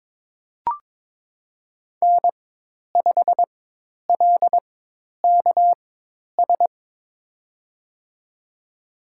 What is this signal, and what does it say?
Morse code, heard as a keyed tone, about 22 wpm, spaced as for 10 wpm, 700 Hz. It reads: N5LKS
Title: USA Call Signs - 5 Characters - 4x Character Spacing - 22wpm